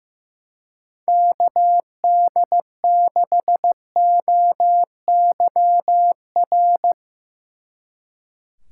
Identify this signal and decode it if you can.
Morse code, heard as a keyed tone, about 15 wpm, 700 Hz. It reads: KD6OYR
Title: Speed Racing - USA 2x3 Call Signs 15wpm